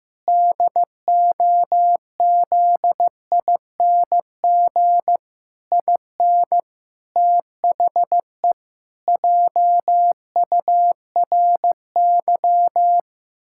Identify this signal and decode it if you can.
Morse code, heard as a keyed tone, about 15 wpm, 700 Hz. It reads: DOZING IN THE JURY
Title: Pangram Challenge v2 - 15wpm